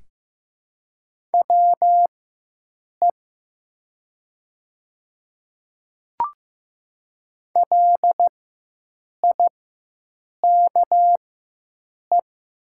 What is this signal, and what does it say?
Morse code, heard as a keyed tone, about 15 wpm, spaced as for 7 wpm, 700 Hz. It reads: WE LIKE
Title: 4x Character Spacing - Top 200 Words - 15wpm